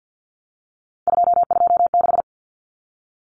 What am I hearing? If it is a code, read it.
Morse code, heard as a keyed tone, about 50 wpm, 700 Hz. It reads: 226
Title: Common Cut Numbers - 3-Digit 50wpm